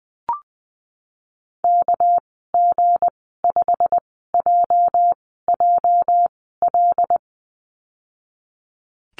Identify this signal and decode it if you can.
Morse code, heard as a keyed tone, about 19 wpm, spaced as for 14 wpm, 700 Hz. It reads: KG5JJL
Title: 207 Lesson 33g - Letter j - Call Signs 20wpm